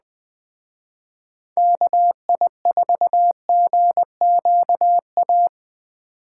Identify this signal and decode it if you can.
Morse code, heard as a keyed tone, about 21 wpm, 700 Hz. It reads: KI4GQA
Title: Speed Racing - USA 2x3 Call Signs 17wpm